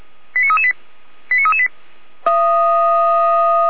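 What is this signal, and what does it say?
Five-tone selective call (ZVEI), heard as quick runs of five tones, tones of about 70 ms. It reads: 89338 89338
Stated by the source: Five-tone ZVEI selective call received on an SDR.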